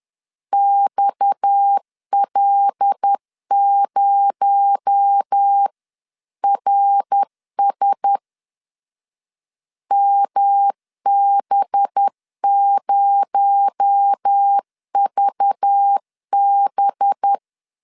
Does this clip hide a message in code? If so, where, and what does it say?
Yes: Morse code from 0.5 to 17.3 s, heard as a keyed tone, about 10 wpm, 800 Hz. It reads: XL0RS MB0VB